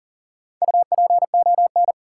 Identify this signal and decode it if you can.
Morse code, heard as a keyed tone, about 40 wpm, 700 Hz. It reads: UPON
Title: Sentences from Top 1100 Words - New Word Drill Alt - 40wpm